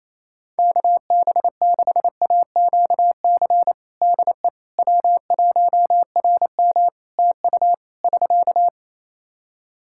Morse code, heard as a keyed tone, about 27 wpm, 700 Hz. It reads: KB6AQC DE W1RM TU <SK>
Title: Practice QSO - Exchange 4 - Ending - 28wpm